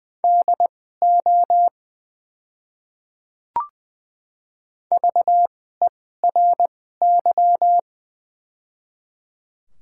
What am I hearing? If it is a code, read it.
Morse code, heard as a keyed tone, about 20 wpm, spaced as for 14 wpm, 700 Hz. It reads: DO VERY
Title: 2x Character Spacing - Top 300 Words - 20wpm